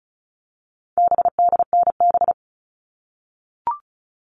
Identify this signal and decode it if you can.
Morse code, heard as a keyed tone, about 34 wpm, 700 Hz. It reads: BDNB
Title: All Cut Numbers - 4-Digit 35wpm